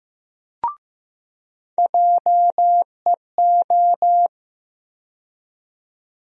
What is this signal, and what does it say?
Morse code, heard as a keyed tone, about 15 wpm, 700 Hz. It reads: JEO
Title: USA Call Sign Postfixes - 3 Letters 15wpm